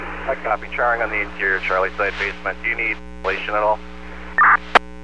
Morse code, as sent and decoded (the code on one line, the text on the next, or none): none